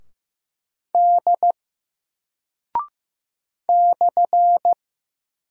Morse code, heard as a keyed tone, about 15 wpm, 700 Hz. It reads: D /
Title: Single Character (Letter:Number:Punctuation:Prosign) 15wpm